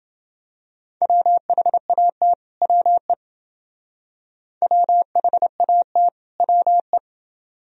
Morse code, heard as a keyed tone, about 28 wpm, 700 Hz. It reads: WHAT WE WHAT WE